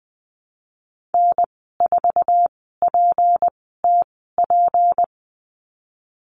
Morse code, heard as a keyed tone, about 19 wpm, 700 Hz. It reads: N 4 P T P
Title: 154 Lesson 25g - Letter p - Call Signs 20wpm